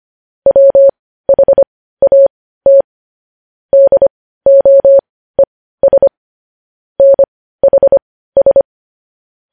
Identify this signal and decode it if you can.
Morse code, heard as a keyed tone, about 25 wpm, spaced as for 15 wpm, 550 Hz. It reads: WHAT DOES NHS
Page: news headlines in morse code at 15 wpm